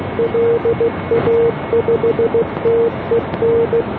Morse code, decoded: LA5TEN